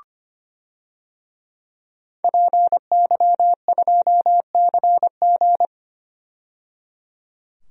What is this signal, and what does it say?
Morse code, heard as a keyed tone, about 25 wpm, 700 Hz. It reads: PY2CG
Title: DXCC-100 Call Signs - 5 Characters - 25wpm